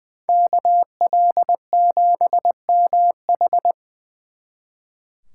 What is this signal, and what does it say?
Morse code, heard as a keyed tone, about 24 wpm, 700 Hz. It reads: KL7MH
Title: Speed Racing - USA 2x2 Call Signs 20wpm